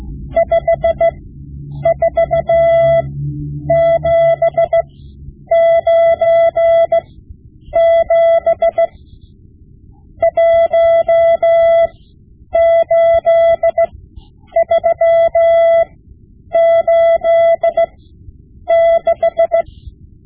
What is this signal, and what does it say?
Morse code, heard as a keyed tone, about 16 wpm, spaced as for 9 wpm, 670 Hz. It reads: 54797 18386
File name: Morse Lessons / Morse Lessons / Tape-5B